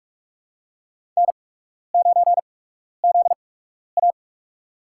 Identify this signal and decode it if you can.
Morse code, heard as a keyed tone, about 42 wpm, spaced as for 12 wpm, 700 Hz. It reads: N9ZA
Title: USA Call Signs - 4 Characters - 8x Character Spacing - 45wpm